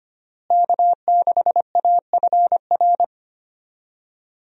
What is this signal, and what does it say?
Morse code, heard as a keyed tone, about 25 wpm, 700 Hz. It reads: K6AFR
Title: USA Call Signs - 5 Characters - Repeated 3x - 25wpm